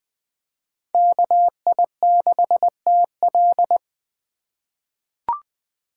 Morse code, heard as a keyed tone, about 21 wpm, 700 Hz. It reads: KI6TL
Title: USA Call Signs - 5 Characters - 3x Character Spacing - 20wpm